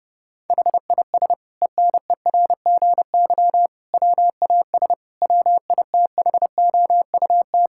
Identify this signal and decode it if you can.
Morse code, heard as a keyed tone, about 30 wpm, 700 Hz. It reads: HIS ENERGY WAS WITHOUT
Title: Sentences from Top 1000 Words - Review Part 6 - 30wpm